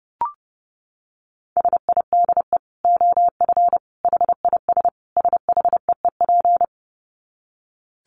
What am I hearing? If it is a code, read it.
Morse code, heard as a keyed tone, about 30 wpm, 700 Hz. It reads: SIDE OF HIS SHEEP
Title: Dit Challenge 30wpm